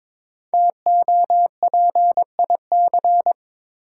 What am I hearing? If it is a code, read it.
Morse code, heard as a keyed tone, about 22 wpm, 700 Hz. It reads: TOPIC
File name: 5 Letter Words - First 3 Letters Given - 22wpm